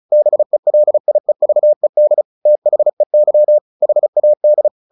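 Morse code, heard as a keyed tone, about 34 wpm, 600 Hz. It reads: BELIEVED THEY HAD
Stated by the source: Morse practice at 35 wpm from CW med Gnister, sending Ken Follett's novel Never in English.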